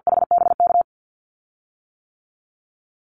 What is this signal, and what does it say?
Morse code, heard as a keyed tone, about 50 wpm, 700 Hz. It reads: HBK
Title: USA Call Signs 50wpm